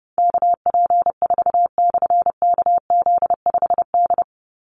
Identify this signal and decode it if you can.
Morse code, heard as a keyed tone, about 30 wpm, 700 Hz. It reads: KP4/KZ5D
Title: Call Signs - 30wpm